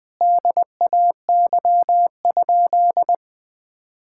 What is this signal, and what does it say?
Morse code, heard as a keyed tone, about 20 wpm, 700 Hz. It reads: DAY?
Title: Sentences from Top 1000 Words - Review Part 5 - 20wpm